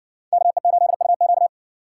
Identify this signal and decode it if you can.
Morse code, heard as a keyed tone, about 55 wpm, 700 Hz. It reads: KE8UQ